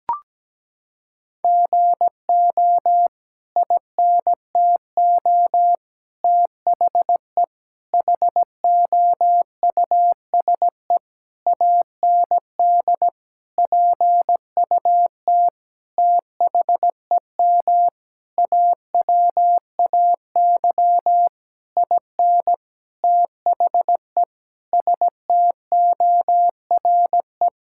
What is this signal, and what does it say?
Morse code, heard as a keyed tone, about 17 wpm, 700 Hz. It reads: GO INTO THE HOUSE AND PUT THEM AWAY IN THE STORE
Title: Sentences from Top 1000 Words - Review Part 5 - 17wpm